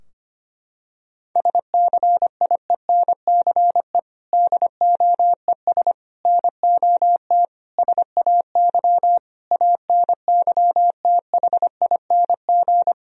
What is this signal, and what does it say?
Morse code, heard as a keyed tone, about 25 wpm, 700 Hz. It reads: SCIENCE DOES NOT SAY ANYTHING
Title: Sentences from Top 1000 Words - Review Part 2 - 25wpm